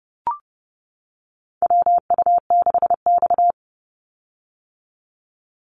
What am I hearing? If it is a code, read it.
Morse code, heard as a keyed tone, about 30 wpm, 700 Hz. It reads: WU6X
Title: Call Signs - 30wpm